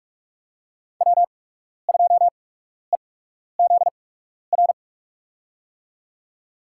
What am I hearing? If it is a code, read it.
Morse code, heard as a keyed tone, about 47 wpm, spaced as for 12 wpm, 700 Hz. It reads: W2EZR
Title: USA Call Signs - 5 Characters - 8x Character Spacing - 45wpm